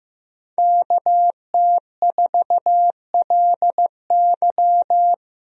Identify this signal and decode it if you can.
Morse code, heard as a keyed tone, about 15 wpm, 700 Hz. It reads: KT4LY
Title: USA Call Signs - 5 Characters - Repeated 2x - 15wpm